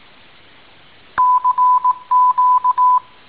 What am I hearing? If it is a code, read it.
Morse code, heard as a keyed tone, about 18 wpm, 1000 Hz. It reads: CQ
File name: tin-upgrade / QSO 2019 01 05 .mp3